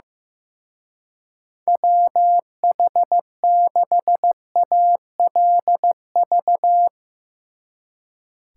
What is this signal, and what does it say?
Morse code, heard as a keyed tone, about 15 wpm, 700 Hz. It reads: WH6ALV
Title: Speed Racing - USA 2x3 Call Signs 15wpm